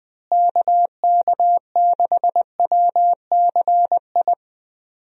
Morse code, heard as a keyed tone, about 20 wpm, 700 Hz. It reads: KK6WCI
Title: USA Call Signs - 6 Characters - Repeated 2x - 20wpm